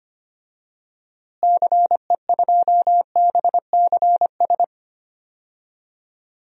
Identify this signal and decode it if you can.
Morse code, heard as a keyed tone, about 25 wpm, 700 Hz. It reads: CE2BCS
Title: DXCC-100 Call Signs - 6 Characters - Repeated 2x - 25wpm